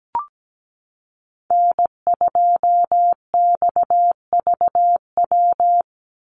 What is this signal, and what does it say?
Morse code, heard as a keyed tone, about 17 wpm, 700 Hz. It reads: N2XVW
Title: USA Call Signs 17wpm